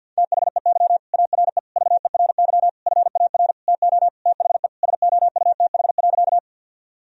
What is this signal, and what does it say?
Morse code, heard as a keyed tone, about 51 wpm, 700 Hz. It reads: THEY ARE VERY FAR TO THE SOUTH.
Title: Sentences from Top 1000 Words - Review Part 2 - 50wpm